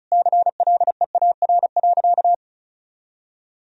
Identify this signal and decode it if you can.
Morse code, heard as a keyed tone, about 36 wpm, 700 Hz. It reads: CLEAR.
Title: Sentences from Top 1000 Words - Review Part 1 - 35wpm